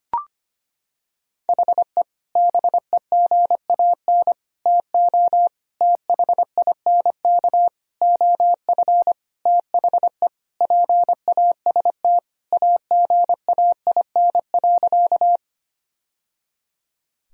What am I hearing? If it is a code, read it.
Morse code, heard as a keyed tone, about 25 wpm, 700 Hz. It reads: HE BEGAN TO THINK OF THE PAST AGAIN.
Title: Sentences from Top 1000 Words - Review Part 3 - 25wpm